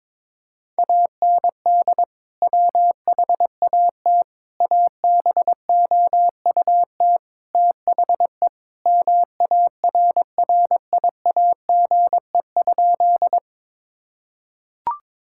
Morse code, heard as a keyed tone, about 21 wpm, 700 Hz. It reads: AND WHAT ABOUT THE MARRIAGE?
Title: Sentences from Top 1000 Words - Review Part 5 - 22wpm